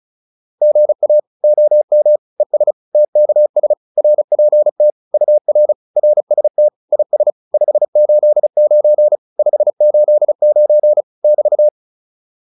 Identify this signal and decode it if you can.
Morse code, heard as a keyed tone, about 34 wpm, 600 Hz. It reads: GA OM ES TKS RPT UR RST IS 589 589 <BT>
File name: QSO203 35